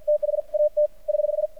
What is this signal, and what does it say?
Morse code, heard as a keyed tone, about 51 wpm, spaced as for 43 wpm, 600 Hz. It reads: THAT <HH>
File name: archived / CRSnet